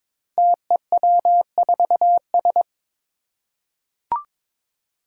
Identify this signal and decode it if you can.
Morse code, heard as a keyed tone, about 22 wpm, 700 Hz. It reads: TEW4S
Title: Five Challenging Letter-Number Combinations 22wpm